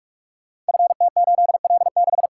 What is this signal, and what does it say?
Morse code, heard as a keyed tone, about 44 wpm, 700 Hz. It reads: FT8L6
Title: Code Groups - 5 Letter:Numbers - 45wpm